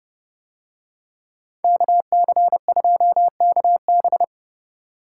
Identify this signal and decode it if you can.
Morse code, heard as a keyed tone, about 30 wpm, 700 Hz. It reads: KC2KB